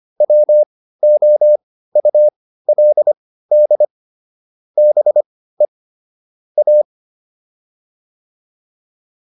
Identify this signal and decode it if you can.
Morse code, heard as a keyed tone, about 25 wpm, spaced as for 15 wpm, 600 Hz. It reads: WOULD BE A
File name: WD204 15